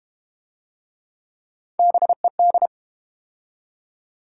Morse code, heard as a keyed tone, about 32 wpm, 700 Hz. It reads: BED